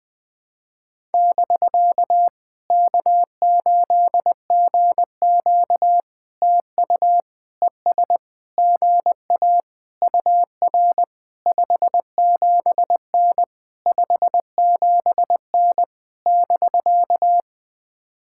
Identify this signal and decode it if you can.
Morse code, heard as a keyed tone, about 21 wpm, 700 Hz. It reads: <BK> K8GQ TU ES GA UR 57N 57N <BK>
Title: POTA QSO Practice - 20wpm